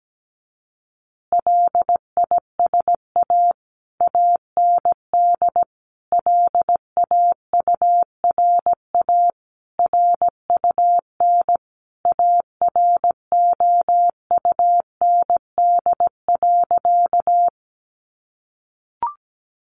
Morse code, heard as a keyed tone, about 17 wpm, 700 Hz. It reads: LISA AND LAURA RUN AROUND.
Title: Additional Easy Phrases 17wpm